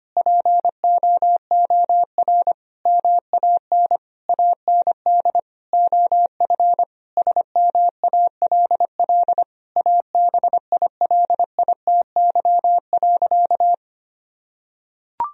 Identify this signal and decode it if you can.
Morse code, heard as a keyed tone, about 25 wpm, 700 Hz. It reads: POOR MAN AND OF SMALL ABILITY.
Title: Sentences from Top 1000 Words - Review Part 3 - 25wpm